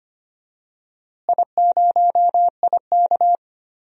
Morse code, heard as a keyed tone, about 25 wpm, 700 Hz. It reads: I0IK
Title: DXCC-100 Call Signs - 4 Characters - Repeated 3x - 25wpm